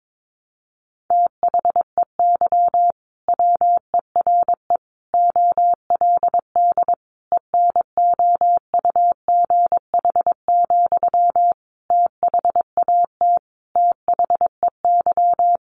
Morse code, heard as a keyed tone, about 22 wpm, 700 Hz. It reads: THEY WERE OLD ENOUGH, THAT THEY
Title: Sentences from Top 800 Words 22wpm